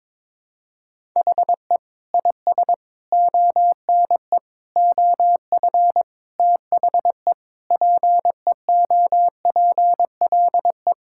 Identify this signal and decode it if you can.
Morse code, heard as a keyed tone, about 22 wpm, 700 Hz. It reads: HE IS ONE OF THE PEOPLE
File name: practice / 22wpm / Sentences with 4-Word Phrases - 22wpm